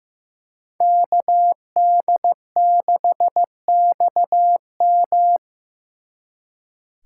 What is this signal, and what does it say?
Morse code, heard as a keyed tone, about 18 wpm, 700 Hz. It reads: KD6XM
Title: Speed Racing - USA 2x2 Call Signs 15wpm